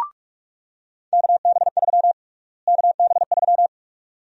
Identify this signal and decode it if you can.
Morse code, heard as a keyed tone, about 47 wpm, 700 Hz. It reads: KB3 KB3
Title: USA Call Sign Prefixes - Repeated 2x - 45wpm